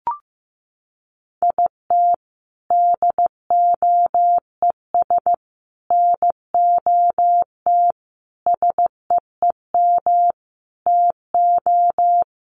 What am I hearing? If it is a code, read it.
Morse code, heard as a keyed tone, about 15 wpm, 700 Hz. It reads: IT DOES NOT SEEM TO
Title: Sets of 5 Words Encore2 15wpm